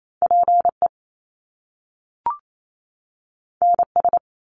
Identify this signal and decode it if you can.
Morse code, heard as a keyed tone, about 28 wpm, 700 Hz. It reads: PE NS
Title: Canadian Provinces and Territories 28wpm